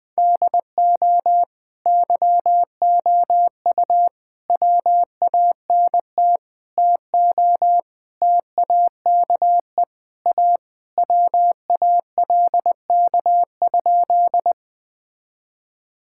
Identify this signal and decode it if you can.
Morse code, heard as a keyed tone, about 21 wpm, 700 Hz. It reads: DO YOU WANT TO TAKE A WALK?
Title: Sentences with 3-Word Phrases - 20wpm